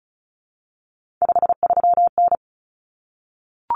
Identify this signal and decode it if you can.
Morse code, heard as a keyed tone, about 34 wpm, 700 Hz. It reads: 53N
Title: Common Cut Numbers - 3-Digit 35wpm